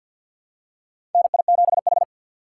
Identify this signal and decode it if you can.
Morse code, heard as a keyed tone, about 52 wpm, 700 Hz. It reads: NI7H
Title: USA Call Signs - 3x Character Spacing - 50wpm